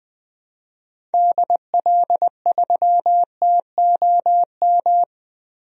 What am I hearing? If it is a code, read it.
Morse code, heard as a keyed tone, about 21 wpm, 700 Hz. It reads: DL3TOM
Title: DXCC-100 Call Signs - 6 Characters - 20wpm